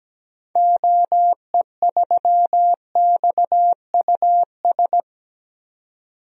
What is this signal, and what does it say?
Morse code, heard as a keyed tone, about 17 wpm, 700 Hz. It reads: OE3XUS
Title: DXCC-100 Call Signs - 6 Characters - 17wpm